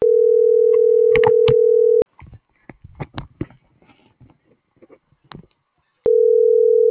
Ringback tone (North America), 440 + 480 Hz.